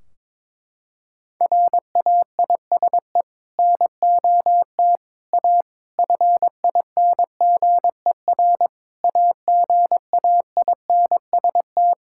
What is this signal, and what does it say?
Morse code, heard as a keyed tone, about 22 wpm, 700 Hz. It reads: RAISE NOT A FINGER AGAINST
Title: Sentences from Top 1600 Words 22wpm